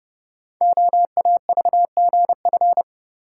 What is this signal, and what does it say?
Morse code, heard as a keyed tone, about 30 wpm, 700 Hz. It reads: OAVGF
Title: Code Groups - 5 Letters - 30wpm